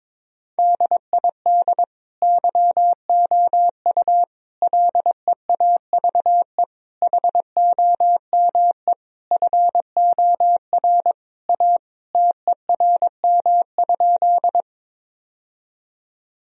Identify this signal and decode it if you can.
Morse code, heard as a keyed tone, about 22 wpm, 700 Hz. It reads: DID YOU LEAVE HOME FOR A TERM?
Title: Sentences from Top 1000 Words - Review Part 4 - 22wpm